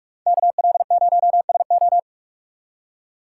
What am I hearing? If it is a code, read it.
Morse code, heard as a keyed tone, about 44 wpm, 700 Hz. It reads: KL0SO